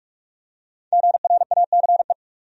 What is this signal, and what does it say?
Morse code, heard as a keyed tone, about 42 wpm, 700 Hz. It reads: GRACE
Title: USA Common Names - Max 5 Letters - 45wpm